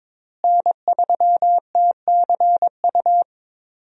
Morse code, heard as a keyed tone, about 21 wpm, 700 Hz. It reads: N3TCU
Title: USA Call Signs - 5 Characters - Repeated 2x - 22wpm